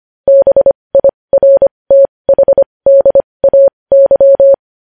Morse code, heard as a keyed tone, about 25 wpm, spaced as for 20 wpm, 550 Hz. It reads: BIRTHDAY
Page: News Headlines in Morse Code at 20 WPM